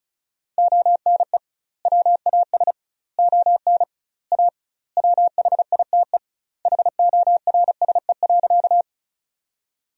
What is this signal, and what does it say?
Morse code, heard as a keyed tone, about 34 wpm, 700 Hz. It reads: ONE WAS ON A WHITE HORSE.
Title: 2x Word-Spacing - Sentences from Top 300 Words - 35wpm